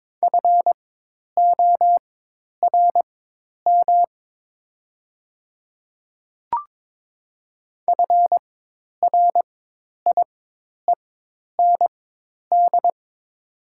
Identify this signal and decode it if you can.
Morse code, heard as a keyed tone, about 22 wpm, spaced as for 10 wpm, 700 Hz. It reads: FORM FRIEND
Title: 4x Character Spacing - Top 300 Words - 22wpm